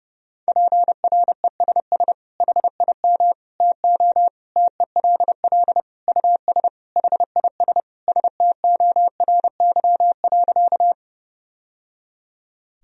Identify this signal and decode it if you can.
Morse code, heard as a keyed tone, about 30 wpm, 700 Hz. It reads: PRESS HIM TO TELL US HIS STORY.
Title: Sentences from Top 1000 Words - Review Part 1 - 30wpm